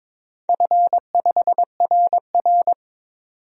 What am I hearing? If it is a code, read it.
Morse code, heard as a keyed tone, about 22 wpm, 700 Hz. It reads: F5RR
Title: DXCC-100 Call Signs - 4 Characters - 22wpm